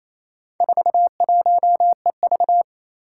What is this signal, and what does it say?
Morse code, heard as a keyed tone, about 28 wpm, 700 Hz. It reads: 41EV
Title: Four Challenging Letter-Number Combinations 28wpm